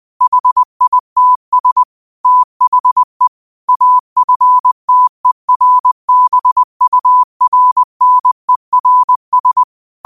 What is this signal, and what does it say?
Morse code, heard as a keyed tone, about 20 wpm, 1000 Hz. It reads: HITS THE AFTERBURNERS